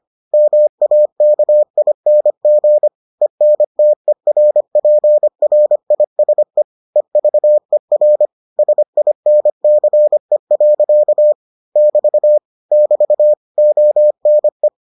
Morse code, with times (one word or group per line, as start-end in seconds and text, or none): 0.3-2.9 MAKING
3.2-6.6 ENTERPRISE
7.0-8.3 EVER
8.6-11.3 SINCE.
11.8-12.4 <BT>
12.7-13.3 <BT>
13.6-14.7 ONE